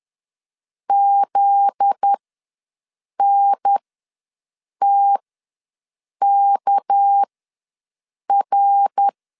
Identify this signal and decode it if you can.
Morse code, heard as a keyed tone, about 10 wpm, spaced as for 6 wpm, 800 Hz. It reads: ZNTKR
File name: LEZIONI / 9 286 ws 10wpm 4032